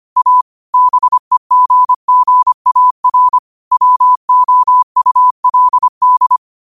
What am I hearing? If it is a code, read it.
Morse code, heard as a keyed tone, about 25 wpm, 1000 Hz. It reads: A BEGGAR WOULD
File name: qotdfiles / 25wpm